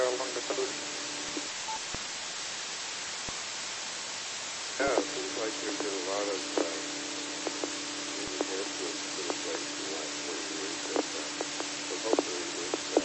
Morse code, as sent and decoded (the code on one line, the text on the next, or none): none